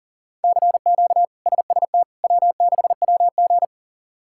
Morse code, heard as a keyed tone, about 40 wpm, 700 Hz. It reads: CQ SST W6WG
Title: K1USN SST QSO Practice - 40wpm